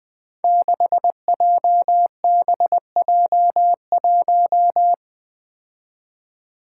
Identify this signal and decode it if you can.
Morse code, heard as a keyed tone, about 20 wpm, 700 Hz. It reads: 6JBJ1
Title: Five Challenging Letter-Number Combinations 20wpm